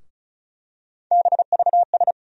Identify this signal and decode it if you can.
Morse code, heard as a keyed tone, about 36 wpm, 700 Hz. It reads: BVS
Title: USA Call Sign Postfixes - 3 Letters 35wpm